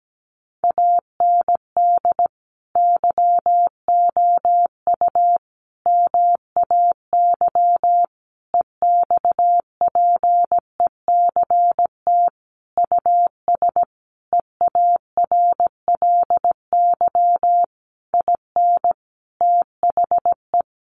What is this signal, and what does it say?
Morse code, heard as a keyed tone, about 17 wpm, 700 Hz. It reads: AND YOU MAY EXPECT US EARLY IN THE
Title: Sentences from Top 1500 Words 17wpm